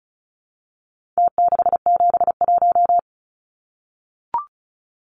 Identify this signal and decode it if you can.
Morse code, heard as a keyed tone, about 34 wpm, 700 Hz. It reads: T671
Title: Numbers - Contest Style - 4-Digit 35wpm